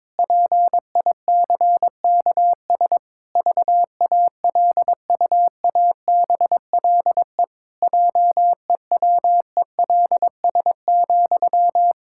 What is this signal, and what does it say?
Morse code, heard as a keyed tone, about 22 wpm, 700 Hz. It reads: PICKS VALUABLE JEWELS,
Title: Pangram Challenge v2 - 22wpm